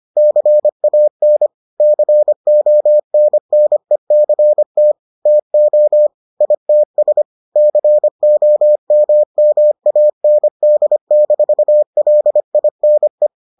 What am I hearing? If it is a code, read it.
Morse code, heard as a keyed tone, about 25 wpm, 600 Hz. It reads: CAN CONNECT TO ITS COMMAND-LINE